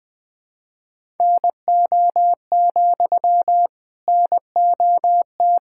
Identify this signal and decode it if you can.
Morse code, heard as a keyed tone, about 20 wpm, 700 Hz. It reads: NO, NOT
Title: Sentences from Top 1000 Words - Review Part 6 - 20wpm